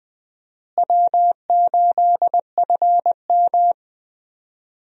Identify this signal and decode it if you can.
Morse code, heard as a keyed tone, about 20 wpm, 700 Hz. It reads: W8FM
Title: USA Call Signs - 4 Characters - Repeated 2x - 20wpm